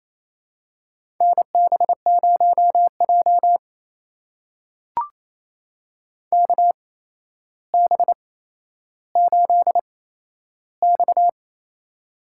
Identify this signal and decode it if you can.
Morse code, heard as a keyed tone, about 28 wpm, spaced as for 12 wpm, 700 Hz. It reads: NB0J KB8X
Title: USA Call Signs - 8x Character Spacing - 28wpm